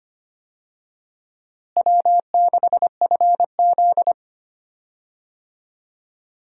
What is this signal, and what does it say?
Morse code, heard as a keyed tone, about 25 wpm, 700 Hz. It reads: W6FZ
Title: USA Call Signs - 4 Characters - Repeated 2x - 25wpm